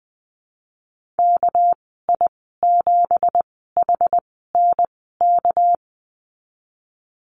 Morse code, heard as a keyed tone, about 20 wpm, spaced as for 15 wpm, 700 Hz. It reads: KI7HNK
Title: 213 Lesson 34g - Number 8 - Call Signs 20wpm